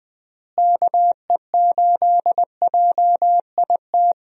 Morse code, heard as a keyed tone, about 20 wpm, 700 Hz. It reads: KE8JIT
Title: USA Call Signs - 6 Characters - Repeated 2x - 20wpm